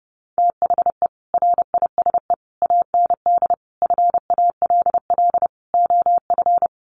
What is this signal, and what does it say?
Morse code, heard as a keyed tone, about 30 wpm, 700 Hz. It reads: THE RISE AND FALL OF
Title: Sets of 5 Words Encore2 30wpm